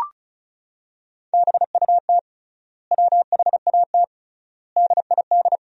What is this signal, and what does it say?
Morse code, heard as a keyed tone, about 36 wpm, 700 Hz. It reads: BUT WHAT DID
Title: 3x Word-Spacing - Sentences from Top 100 Words - 35wpm